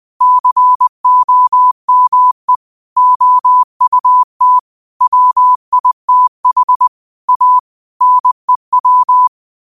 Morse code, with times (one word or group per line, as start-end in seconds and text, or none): 0.2-2.6 COME
3.0-4.6 OUT
5.0-6.9 WITH
7.3-7.6 A
8.0-9.3 NEW